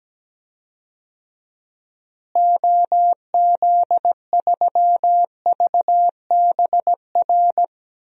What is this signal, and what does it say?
Morse code, heard as a keyed tone, about 17 wpm, 700 Hz. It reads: OZ3VBR
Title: DXCC-100 Call Signs - 6 Characters - Repeated 2x - 17wpm